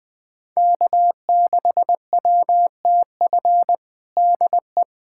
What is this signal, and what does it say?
Morse code, heard as a keyed tone, about 20 wpm, 700 Hz. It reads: K6WTF DE
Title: USA Callsign DE Callsign - 5 Characters - 20wpm